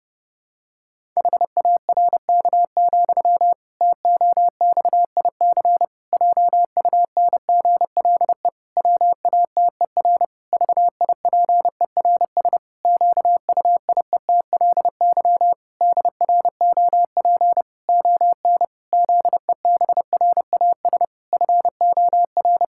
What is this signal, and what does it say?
Morse code, heard as a keyed tone, about 30 wpm, 700 Hz. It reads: HARK, TOXIC JUNGLE WATER VIPERS QUIETLY DROP ON ZEBRAS FOR
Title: Pangram Challenge v2 - 30wpm